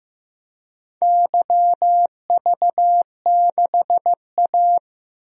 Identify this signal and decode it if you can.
Morse code, heard as a keyed tone, about 15 wpm, 700 Hz. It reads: YV6A
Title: DXCC-100 Call Signs - 4 Characters - Repeated 2x - 15wpm